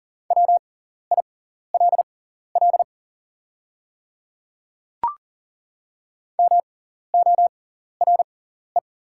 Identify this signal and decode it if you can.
Morse code, heard as a keyed tone, about 37 wpm, spaced as for 14 wpm, 690 Hz. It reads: WILL MORE
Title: 6x Character Spacing - Top 100 Words - 40wpm